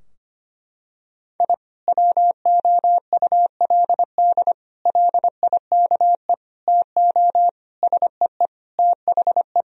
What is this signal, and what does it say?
Morse code, heard as a keyed tone, about 25 wpm, 700 Hz. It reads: I WOULD LIKE TO SEE THE
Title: Sentences from Top 1000 Words - Review Part 4 - 25wpm